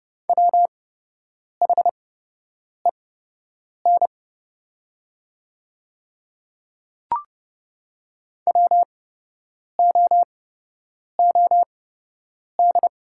Morse code, heard as a keyed tone, about 30 wpm, spaced as for 8 wpm, 700 Hz. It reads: WHEN WOOD
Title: 8x Character Spacing - Top 500 Words - 30wpm